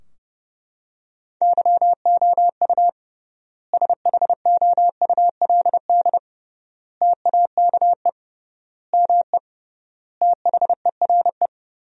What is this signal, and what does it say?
Morse code, heard as a keyed tone, about 32 wpm, 700 Hz. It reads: YOU SHOULD TAKE ME THERE
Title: 3x Word-Spacing - Sentences from Top 300 Words - 30wpm